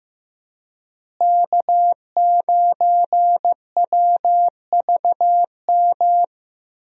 Morse code, heard as a keyed tone, about 16 wpm, 700 Hz. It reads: K9WVM
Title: USA Call Signs - 5 Characters - Repeated 3x - 15wpm